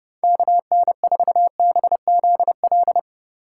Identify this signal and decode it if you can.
Morse code, heard as a keyed tone, about 38 wpm, 700 Hz. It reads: KN4BZL